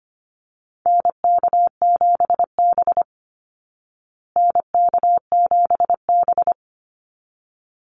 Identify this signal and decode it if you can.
Morse code, heard as a keyed tone, about 25 wpm, 700 Hz. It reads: NK7B NK7B
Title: Call Signs Repeated 3x - 25wpm